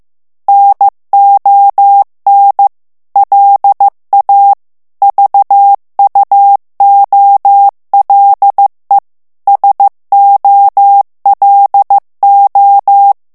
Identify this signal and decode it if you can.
Morse code, heard as a keyed tone, about 15 wpm, 800 Hz. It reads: NON LA VUOLE SOLO
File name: ITALIEN texte clair 1 vitesse 14 mots minute